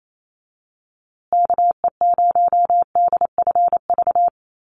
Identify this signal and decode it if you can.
Morse code, heard as a keyed tone, about 28 wpm, 700 Hz. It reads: KE0DFV
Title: USA Call Signs Repeated 3x 28wpm